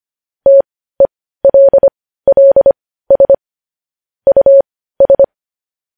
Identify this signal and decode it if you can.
Morse code, heard as a keyed tone, about 25 wpm, spaced as for 15 wpm, 550 Hz. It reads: TELLS US
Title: News Headlines at 15